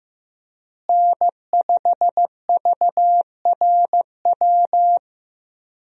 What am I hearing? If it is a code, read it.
Morse code, heard as a keyed tone, about 15 wpm, 700 Hz. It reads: N5VRW
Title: USA Call Signs - 5 Characters - Repeated 3x - 15wpm